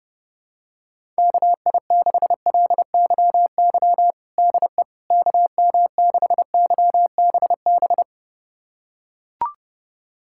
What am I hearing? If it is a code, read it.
Morse code, heard as a keyed tone, about 30 wpm, 700 Hz. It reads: KI6LYY DE KM6YBB
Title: USA Callsign DE Callsign - 6 Characters - 30wpm